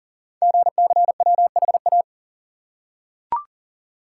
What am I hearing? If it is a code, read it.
Morse code, heard as a keyed tone, about 41 wpm, 700 Hz. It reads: GCWHA
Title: Code Groups - 5 Letter:Numbers - 40wpm